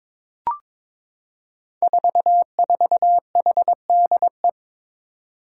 Morse code, heard as a keyed tone, about 22 wpm, 700 Hz. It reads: 44HDE
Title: Five Challenging Letter-Number Combinations 22wpm